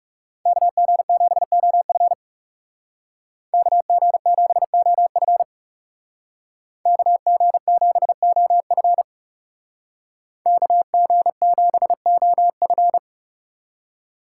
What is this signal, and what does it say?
Morse code, heard as a keyed tone, about 37 wpm, 700 Hz. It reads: KG7OF KG7OF KG7OF KG7OF
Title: Speed Racing - USA 2x2 Call Signs 30wpm